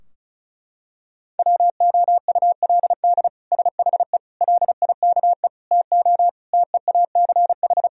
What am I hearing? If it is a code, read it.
Morse code, heard as a keyed tone, about 34 wpm, 700 Hz. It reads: WOULD SHE LIKE TO TEACH
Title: Sentences from Top 1000 Words - Review Part 2 - 35wpm